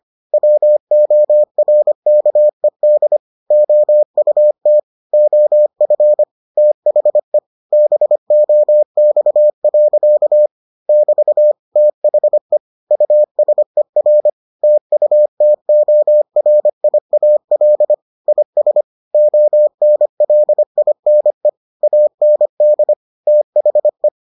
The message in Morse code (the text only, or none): WORKED OUT OF THE BOX. <BT> THE USER TUTORIAL IS ONLINE AND THE